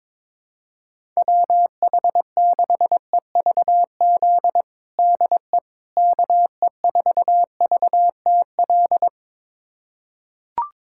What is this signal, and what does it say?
Morse code, heard as a keyed tone, about 22 wpm, 700 Hz. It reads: WH6EVZ DE KE4VTL
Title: USA Callsign DE Callsign - 6 Characters - 22wpm